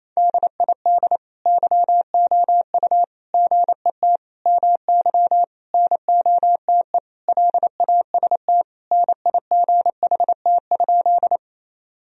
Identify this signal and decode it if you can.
Morse code, heard as a keyed tone, about 28 wpm, 700 Hz. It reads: DID YOU GET MY NOTE LAST NIGHT?
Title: Sentences from Top 1000 Words - Review Part 2 - 28wpm